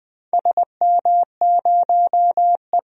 Morse code, heard as a keyed tone, about 19 wpm, 700 Hz. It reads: SM0E
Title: DXCC-100 Call Signs - 4 Characters - Repeated 3x - 20wpm